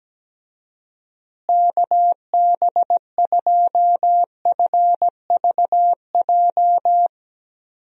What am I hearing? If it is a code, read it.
Morse code, heard as a keyed tone, about 17 wpm, 700 Hz. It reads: KB2FVJ
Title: USA Call Signs - 6 Characters - Repeated 2x - 17wpm